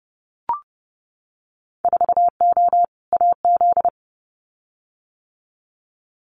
Morse code, heard as a keyed tone, about 31 wpm, 700 Hz. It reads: 4O AZ
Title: Winter Field Day Exchanges Encore 30wpm